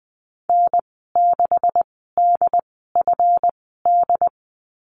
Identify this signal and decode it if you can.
Morse code, heard as a keyed tone, about 20 wpm, spaced as for 15 wpm, 700 Hz. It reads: N6DFD
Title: 140 Lesson 23g - Letter f - Call Signs 20wpm